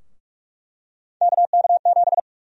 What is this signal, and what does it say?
Morse code, heard as a keyed tone, about 42 wpm, 700 Hz. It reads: KK7
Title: USA Call Sign Prefixes - 45wpm